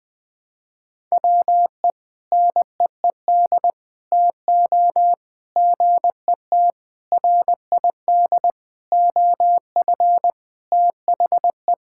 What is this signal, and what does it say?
Morse code, heard as a keyed tone, about 20 wpm, 700 Hz. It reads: WE NEED TO GET RID OF THE
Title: Sentences with 5-Word Phrases - 20wpm